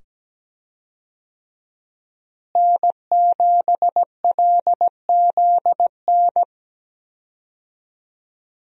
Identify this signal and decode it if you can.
Morse code, heard as a keyed tone, about 17 wpm, 700 Hz. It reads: N7LZN
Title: USA Call Signs - 5 Characters - Repeated 2x - 17wpm